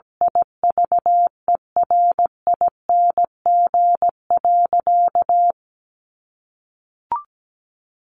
Morse code, heard as a keyed tone, about 17 wpm, 700 Hz. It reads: IVERING.
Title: A Sound Of Thunder 17wpm